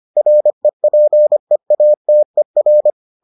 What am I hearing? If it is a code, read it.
Morse code, heard as a keyed tone, about 25 wpm, 600 Hz. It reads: REPEATER